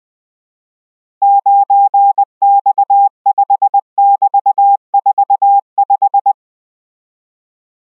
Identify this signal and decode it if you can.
Morse code, heard as a keyed tone, about 20 wpm, 800 Hz. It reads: 9X5<BT>45